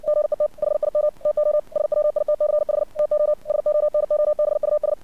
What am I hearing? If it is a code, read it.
Morse code, heard as a keyed tone, about 60 wpm, 600 Hz. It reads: GET HIM TO SWITCH TO SONOBUS
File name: archived / CRSnet